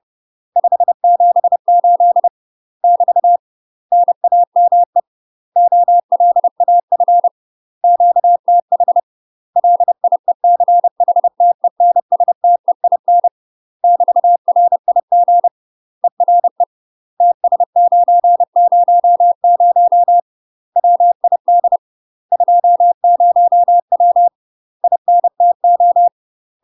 Morse code, heard as a keyed tone, about 30 wpm, 700 Hz. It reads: 578 <BT> NAME OLAF QTH LIECHTENSTEIN <BT>RIG ERE TS900 WID 20W INTO
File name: UK MASTER QSOs TRAINING / 30WPM / UK MASTER QSOs 0020